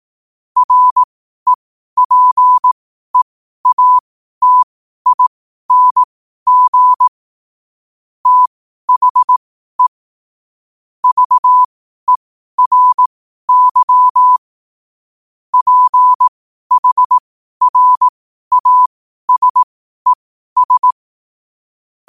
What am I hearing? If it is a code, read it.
Morse code, heard as a keyed tone, about 18 wpm, spaced as for 12 wpm, 1000 Hz. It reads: REPEATING THE VERY PHRASES
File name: qotdfiles / 12wpm